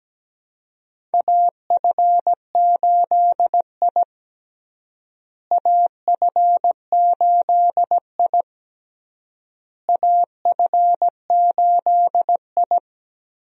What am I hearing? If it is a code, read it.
Morse code, heard as a keyed tone, about 17 wpm, 700 Hz. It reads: AF8I AF8I AF8I
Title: USA Call Signs - 4 Characters - Repeated 3x - 17wpm